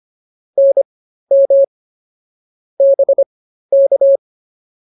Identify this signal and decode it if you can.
Morse code, heard as a keyed tone, about 24 wpm, spaced as for 13 wpm, 550 Hz. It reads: NM BK